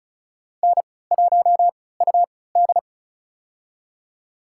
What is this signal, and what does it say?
Morse code, heard as a keyed tone, about 36 wpm, spaced as for 20 wpm, 700 Hz. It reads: N1UD
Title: USA Call Signs - 4 Characters - 3x Character Spacing - 35wpm